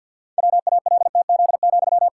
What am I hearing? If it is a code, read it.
Morse code, heard as a keyed tone, about 46 wpm, 700 Hz. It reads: WALTZ,
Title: Pangram Challenge v2 - 50wpm